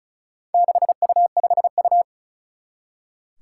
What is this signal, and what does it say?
Morse code, heard as a keyed tone, about 36 wpm, 700 Hz. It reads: 6U5U
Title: Four Challenging Letter-Number Combinations 35wpm